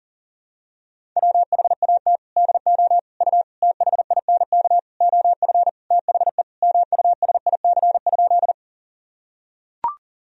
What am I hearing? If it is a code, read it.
Morse code, heard as a keyed tone, about 40 wpm, 700 Hz. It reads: WHAT DO U THINK OF THE MUSIC?
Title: Sentences from Top 300 Words and Abbreviations 40wpm